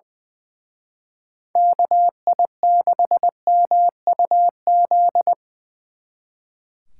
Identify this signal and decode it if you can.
Morse code, heard as a keyed tone, about 20 wpm, 700 Hz. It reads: KI6MUZ
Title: Speed Racing - USA Call Signs 20wpm